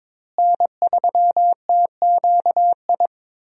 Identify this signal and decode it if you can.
Morse code, heard as a keyed tone, about 22 wpm, 700 Hz. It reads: N3TQI
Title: USA Call Signs - 5 Characters - Repeated 2x - 22wpm